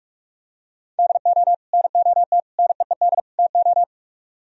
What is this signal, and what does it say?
Morse code, heard as a keyed tone, about 47 wpm, 700 Hz. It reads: DO NOT NEED TO